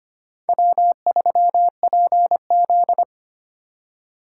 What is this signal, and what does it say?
Morse code, heard as a keyed tone, about 30 wpm, 700 Hz. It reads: W3PZ
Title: Speed Racing - USA 1x2 Call Signs 25wpm